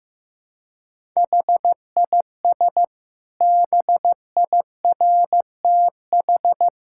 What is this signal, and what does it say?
Morse code, heard as a keyed tone, about 15 wpm, 700 Hz. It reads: HIS BIRTH
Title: Sentences from Top 1000 Words - Review Part 5 - 15wpm